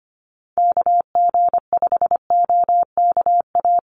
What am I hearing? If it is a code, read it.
Morse code, heard as a keyed tone, about 25 wpm, 700 Hz. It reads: KG5OKA
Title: USA Call Signs 25wpm